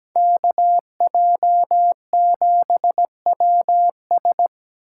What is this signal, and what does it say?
Morse code, heard as a keyed tone, about 20 wpm, 700 Hz. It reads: KJ7WS
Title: Speed Racing - USA Call Signs 17wpm